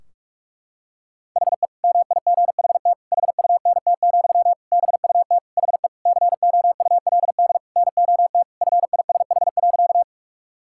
Morse code, heard as a keyed tone, about 47 wpm, 700 Hz. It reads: HE MIGHT HUNT, BUT HE COULD NOT FISH.
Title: Sentences from Top 1600 Words 45wpm